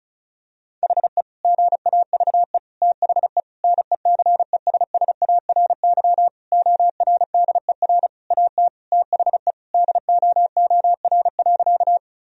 Morse code, heard as a keyed tone, about 34 wpm, 700 Hz. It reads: HE GAVE THE NECESSARY ORDER AT THE DOOR.
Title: Sentences from Top 1000 Words - Review Part 2 - 35wpm